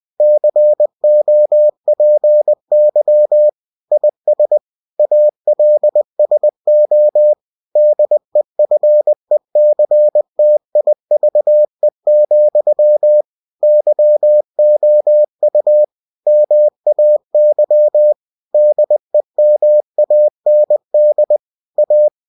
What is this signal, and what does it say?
Morse code, heard as a keyed tone, about 20 wpm, 600 Hz. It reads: COPY IS ALSO DEFECTIVE, YOU MAY DEMAND A
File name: cw-books / winnie-the-pooh / winnie-20wpm-page-0119